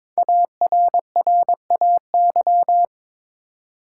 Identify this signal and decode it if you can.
Morse code, heard as a keyed tone, about 22 wpm, 690 Hz. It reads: ARRAY
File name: 5 Letter Words - First 3 Letters Given - 22wpm